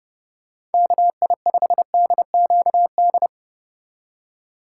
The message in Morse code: KI5DQD

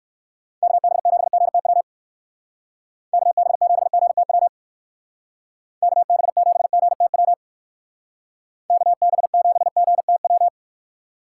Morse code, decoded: KB7GTW KB7GTW KB7GTW KB7GTW

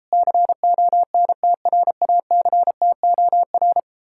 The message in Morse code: CONTRACTOR

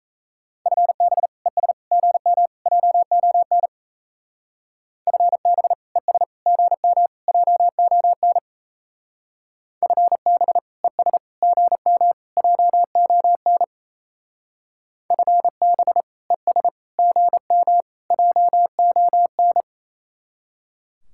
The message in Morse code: FB ES GM JON FB ES GM JON FB ES GM JON FB ES GM JON

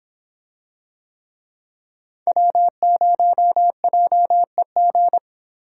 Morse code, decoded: W0JEG

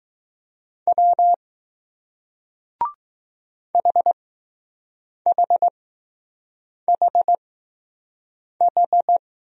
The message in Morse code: W HHHH

